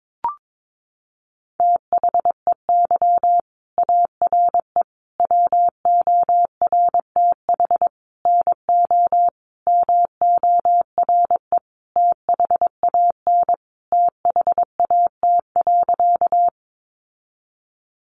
Morse code, heard as a keyed tone, about 22 wpm, 700 Hz. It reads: THEY ARE WORTH NO MORE THAN THAT.
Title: Sentences from Top 800 Words 22wpm